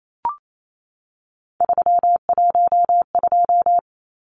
Morse code, heard as a keyed tone, about 28 wpm, 700 Hz. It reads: 312
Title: Numbers - 3-Digit 28wpm